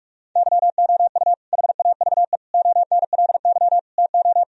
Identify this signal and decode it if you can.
Morse code, heard as a keyed tone, about 41 wpm, 700 Hz. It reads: YOU HAVE ONLY TO
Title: Sentences from Top 1000 Words - Review Part 1 - 45wpm